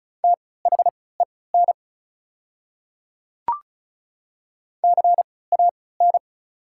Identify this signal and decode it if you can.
Morse code, heard as a keyed tone, about 36 wpm, spaced as for 20 wpm, 690 Hz. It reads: THEN CAN
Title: 3x Character Spacing - Top 100 Words - 35wpm